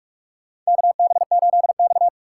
Morse code, heard as a keyed tone, about 46 wpm, 700 Hz. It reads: KB8X
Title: USA Call Signs - 4 Characters - 2x Character Spacing - 45wpm